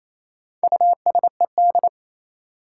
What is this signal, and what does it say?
Morse code, heard as a keyed tone, about 28 wpm, 700 Hz. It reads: USED
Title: Sentences from Top 600 Words - New Word Drill - 28wpm